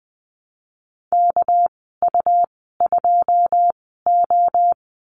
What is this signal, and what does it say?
Morse code, heard as a keyed tone, about 20 wpm, spaced as for 15 wpm, 700 Hz. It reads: KU2O
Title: 200 Lesson 32g - Letter k - Call Signs 20wpm